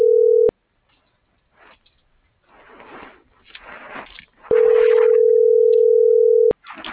Ringback tone (North America), 440 + 480 Hz.